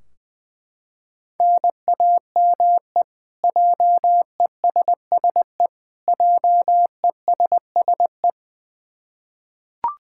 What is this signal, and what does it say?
Morse code, heard as a keyed tone, about 20 wpm, 700 Hz. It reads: NAME JESSE JESSE
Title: K1USN SST QSO Practice 20wpm